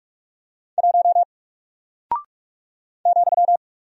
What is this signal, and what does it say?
Morse code, heard as a keyed tone, about 44 wpm, 700 Hz. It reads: 1 ,
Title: Single Character (Letter:Number:Punctuation:Prosign) Alt 45wpm